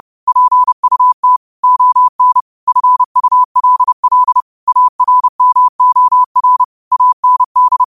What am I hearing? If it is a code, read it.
Morse code, heard as a keyed tone, about 30 wpm, 1000 Hz. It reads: PUT ON FULL ARMOR AND
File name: qotdfiles / 30wpm